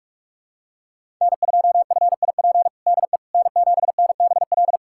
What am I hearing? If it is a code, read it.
Morse code, heard as a keyed tone, about 46 wpm, 700 Hz. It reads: N2FIP DE N7NBL
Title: USA Callsign DE Callsign - 5 Characters - 45wpm